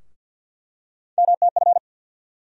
Morse code, heard as a keyed tone, about 47 wpm, 700 Hz. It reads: MTF
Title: USA Call Sign Postfixes - 3 Letters 50wpm